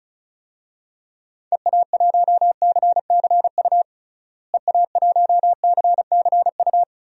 Morse code, heard as a keyed tone, about 36 wpm, 700 Hz. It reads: EA1CCU EA1CCU
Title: DXCC-100 Call Signs - 6 Characters - Repeated 3x - 35wpm